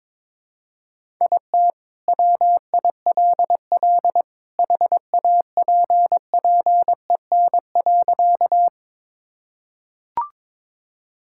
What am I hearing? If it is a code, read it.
Morse code, heard as a keyed tone, about 22 wpm, 700 Hz. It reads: IT WILL HAPPEN.
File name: practice / 22wpm / Sentences with 2-Word Phrases - 22wpm